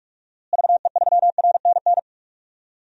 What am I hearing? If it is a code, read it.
Morse code, heard as a keyed tone, about 45 wpm, 700 Hz. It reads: VE3RNN
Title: DXCC-100 Call Signs - 6 Characters - Repeated 2x - 45wpm